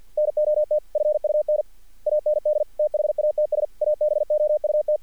Morse code, heard as a keyed tone, about 46 wpm, 600 Hz. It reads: NOT FUN AND THATS ABOUT
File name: archived / CRSnet